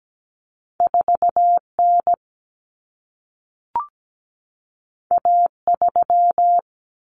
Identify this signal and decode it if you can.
Morse code, heard as a keyed tone, about 17 wpm, 700 Hz. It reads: 4N A3
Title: Common Cut Numbers - 2-Digit 17wpm